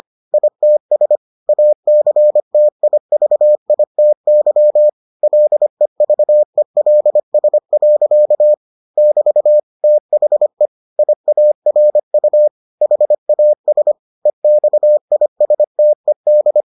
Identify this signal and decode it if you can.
Morse code, heard as a keyed tone, about 25 wpm, 600 Hz. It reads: ITS ACTIVITY LEVELS. <BT> THE IARU HAS EXISTED